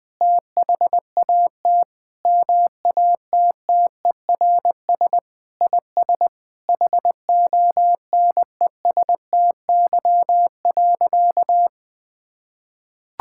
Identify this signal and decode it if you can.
Morse code, heard as a keyed tone, about 21 wpm, 700 Hz. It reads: THAT MATTERS IS HONESTY.
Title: Sentences with 5-Word Phrases - Spoken Beginning - 20wpm